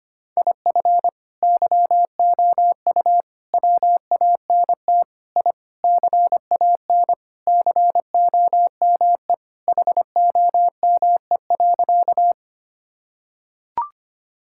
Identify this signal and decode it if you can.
Morse code, heard as a keyed tone, about 25 wpm, 700 Hz. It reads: IF YOU WANT I CAN COME HOME.
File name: practice / 25wpm / Sentences with 3-Word Phrases - 25wpm